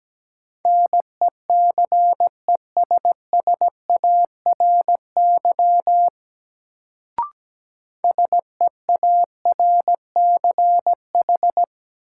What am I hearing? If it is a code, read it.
Morse code, heard as a keyed tone, about 17 wpm, 700 Hz. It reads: NECESSARY SEARCH
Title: Sentences from Top 800 Words - New Word Drill - 17wpm